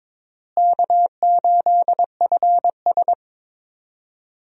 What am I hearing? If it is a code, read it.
Morse code, heard as a keyed tone, about 22 wpm, 700 Hz. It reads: K8FS
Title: USA Call Signs - 4 Characters - Repeated 3x - 22wpm